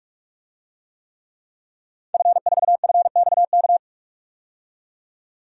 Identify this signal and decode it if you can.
Morse code, heard as a keyed tone, about 42 wpm, 700 Hz. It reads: F4FXK
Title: DXCC-100 Call Signs - 5 Characters - 45wpm